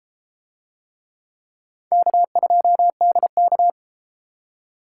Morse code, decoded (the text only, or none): K2DK